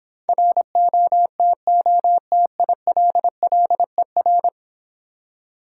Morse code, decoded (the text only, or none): ROTOTILLER